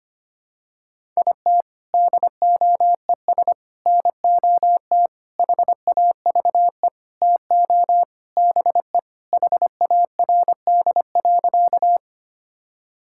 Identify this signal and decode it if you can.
Morse code, heard as a keyed tone, about 25 wpm, 700 Hz. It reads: IT DOES NOT HAVE TO BE HARD.
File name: practice / 25wpm / Sentences with 4-Word Phrases - 25wpm